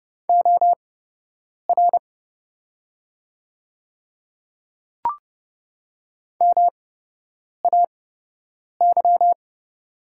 Morse code, heard as a keyed tone, about 30 wpm, spaced as for 8 wpm, 700 Hz. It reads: OR MAY
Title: 8x Character Spacing - Top 100 Words - 30wpm